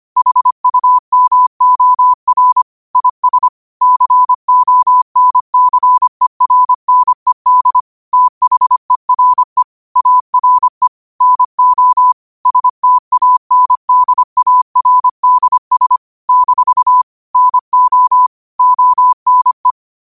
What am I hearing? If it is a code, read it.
Morse code, heard as a keyed tone, about 25 wpm, 1000 Hz. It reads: HUMOR IS CONCERNED THERE ARE NO STANDARDS - NO ONE
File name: qotdfiles / 25wpm